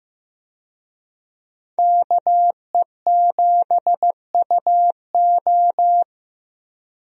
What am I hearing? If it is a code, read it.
Morse code, heard as a keyed tone, about 16 wpm, 700 Hz. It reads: KE7UO
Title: USA Call Signs - 5 Characters - Repeated 3x - 15wpm